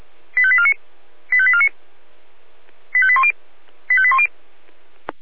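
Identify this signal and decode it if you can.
Five-tone selective call (ZVEI), heard as quick runs of five tones, tones of about 70 ms. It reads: 85849 85849 85810 85810
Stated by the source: ZVEI five-tone selective call received on SDR.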